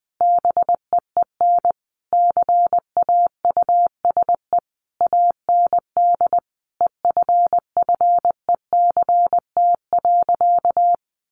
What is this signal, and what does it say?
Morse code, heard as a keyed tone, about 20 wpm, 700 Hz. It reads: BEEN CAUSE AND EFFECT.
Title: Sentences from Top 1000 Words 20wpm